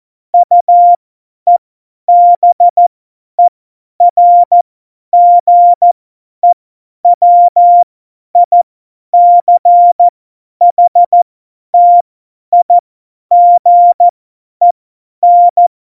Morse code, heard as a keyed tone, about 14 wpm, spaced as for 10 wpm, 700 Hz. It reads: UEBERGEWICHTIGEN
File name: Morsekurs / Witze / Witz 13 March 2018, 50 F70